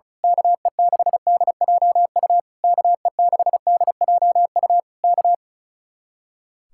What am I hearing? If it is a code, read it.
Morse code, heard as a keyed tone, about 33 wpm, 700 Hz. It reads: KE6DJU KE6DJU K
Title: Speed-Racing Common QSO Phrases - 35wpm